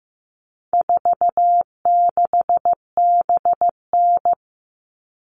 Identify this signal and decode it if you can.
Morse code, heard as a keyed tone, about 15 wpm, 700 Hz. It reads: 46BN
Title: All Cut Numbers - 4-Digit 15wpm